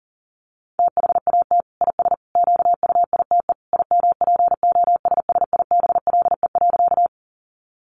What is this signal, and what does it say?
Morse code, heard as a keyed tone, about 40 wpm, 700 Hz. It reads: THAT IS QUITE IMPOSSIBLE.
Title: Sentences from Top 1100 Words 40wpm